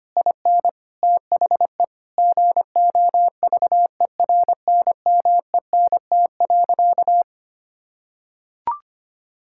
Morse code, heard as a keyed tone, about 25 wpm, 700 Hz. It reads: IN THE GOVERNMENT.
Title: Sentences from Top 1600 Words 25wpm